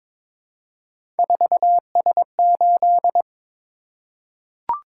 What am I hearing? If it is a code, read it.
Morse code, heard as a keyed tone, about 21 wpm, 700 Hz. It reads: 4S8
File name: practice / 22wpm / Three Letter-Number Combinations - 22wpm